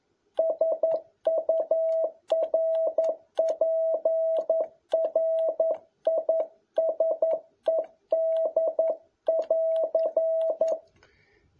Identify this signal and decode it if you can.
Morse code, heard as a keyed tone, about 11 wpm, 650 Hz. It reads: SURPRISED <AR>
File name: jh8cbh / 2021cw